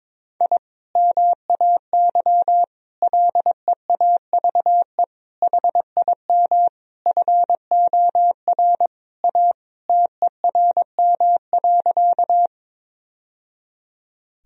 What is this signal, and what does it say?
Morse code, heard as a keyed tone, about 21 wpm, 700 Hz. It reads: I MAY LEAVE HIM FOR A TERM.
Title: Sentences from Top 1000 Words - Review Part 4 - 22wpm